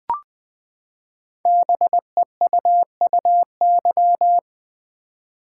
Morse code, heard as a keyed tone, about 20 wpm, 700 Hz. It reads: BEUUY
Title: Code Groups - 5 Letters - 20wpm